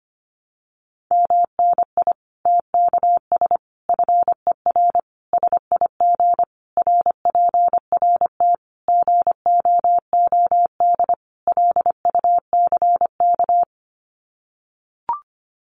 Morse code, heard as a keyed tone, about 25 wpm, 700 Hz. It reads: MNI TKS FER SIG RPRT GOOD LUCK
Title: Common QSO Phrases 25wpm